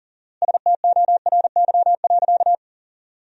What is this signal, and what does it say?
Morse code, heard as a keyed tone, about 40 wpm, 700 Hz. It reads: STORY.
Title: 4x Word-Spacing - Sentences from Top 300 Words - 40wpm